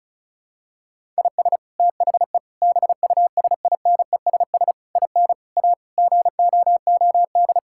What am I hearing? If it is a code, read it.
Morse code, heard as a keyed tone, about 34 wpm, 700 Hz. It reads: IS THE BUSINESS IN A GOOD
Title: Sentences from Top 1000 Words - Review Part 5 - 35wpm